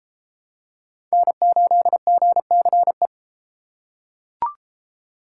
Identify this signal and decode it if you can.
Morse code, heard as a keyed tone, about 30 wpm, 700 Hz. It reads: N8GCE